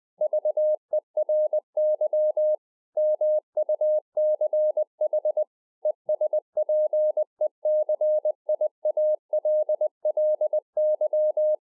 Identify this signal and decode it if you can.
Morse code, heard as a keyed tone, about 20 wpm, 610 Hz. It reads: VERY MUCH ESPECIALLY
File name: 2011-04-mp3 / A SMALL TRANCEIVER 3 20WPM